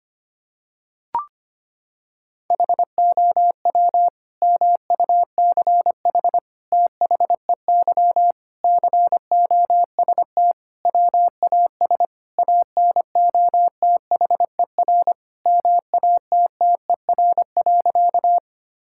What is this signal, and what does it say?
Morse code, heard as a keyed tone, about 25 wpm, 700 Hz. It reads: HOW MUCH THEY COST WAS ANOTHER MATTER.
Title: Sentences from Top 1000 Words - Review Part 5 - 25wpm